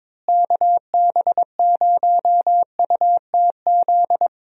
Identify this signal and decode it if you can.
Morse code, heard as a keyed tone, about 22 wpm, 700 Hz. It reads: KB0UTZ
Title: USA Call Signs - 6 Characters - Repeated 3x - 22wpm